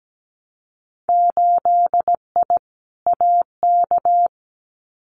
Morse code, heard as a keyed tone, about 17 wpm, 700 Hz. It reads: 8I AK
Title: Winter Field Day Exchanges Encore 17wpm